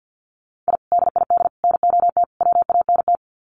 Morse code, heard as a keyed tone, about 50 wpm, 700 Hz. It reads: I DID NOT WANT
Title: Sets of 4 Words 50wpm